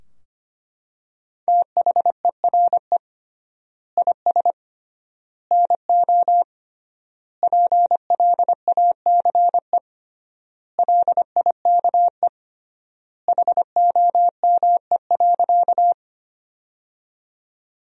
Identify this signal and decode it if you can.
Morse code, heard as a keyed tone, about 25 wpm, 700 Hz. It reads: THERE IS NO PLACE LIKE HOME.
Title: 3x Word-Spacing - Sentences from Top 200 Words - 25wpm